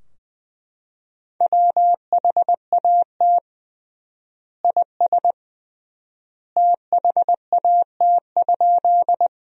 Morse code, heard as a keyed tone, about 21 wpm, 700 Hz. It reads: WHAT IS THAT?
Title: 3x Word-Spacing - Sentences from Top 100 Words - 20wpm